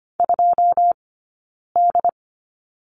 Morse code, heard as a keyed tone, about 25 wpm, spaced as for 9 wpm, 690 Hz. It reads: 2D
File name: course / 25:10wpm / 092 Lesson 16d - Letter c - Two Characters 25wpm